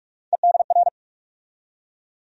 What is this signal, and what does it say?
Morse code, heard as a keyed tone, about 43 wpm, 700 Hz. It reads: EDR
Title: USA Call Sign Postfixes - 3 Letters 45wpm